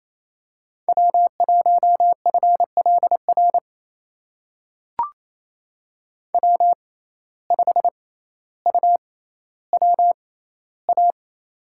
Morse code, heard as a keyed tone, about 28 wpm, spaced as for 14 wpm, 700 Hz. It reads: W1FLR W5UWA
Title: USA Call Signs - 5 Characters - 6x Character Spacing - 28wpm